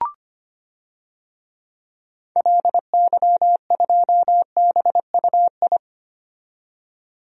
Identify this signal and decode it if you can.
Morse code, heard as a keyed tone, about 25 wpm, 700 Hz. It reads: LY2BUI
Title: DXCC-100 Call Signs - 6 Characters - 25wpm